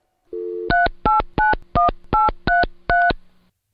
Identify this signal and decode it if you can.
DTMF, heard as telephone touch tones, tones of about 160 ms. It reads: 3891833